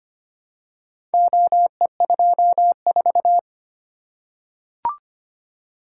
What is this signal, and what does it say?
Morse code, heard as a keyed tone, about 25 wpm, 700 Hz. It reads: OE24
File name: practice / 25wpm / Four Letter-Number Combinations - 25wpm